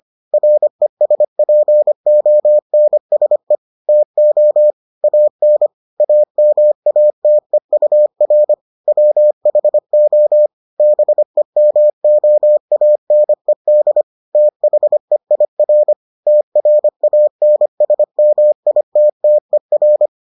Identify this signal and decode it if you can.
Morse code, heard as a keyed tone, about 25 wpm, 600 Hz. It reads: RESPONSE TO AN AMATEUR WHO BEMOANED THEIR TRANSMITTER